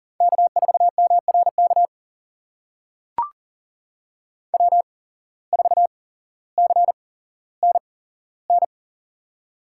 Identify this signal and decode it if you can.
Morse code, heard as a keyed tone, about 40 wpm, spaced as for 17 wpm, 700 Hz. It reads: K4MRK W4CNN
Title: USA Call Signs - 5 Characters - 8x Character Spacing - 40wpm